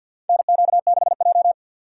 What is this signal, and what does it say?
Morse code, heard as a keyed tone, about 47 wpm, 700 Hz. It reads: NQ6J
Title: USA Call Signs - 4 Characters - 6x Character Spacing - 50wpm